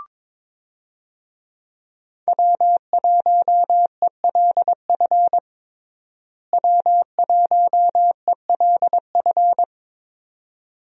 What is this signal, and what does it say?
Morse code, heard as a keyed tone, about 22 wpm, 700 Hz. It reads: W1ELF W1ELF
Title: USA Call Signs - 5 Characters - Repeated 2x - 22wpm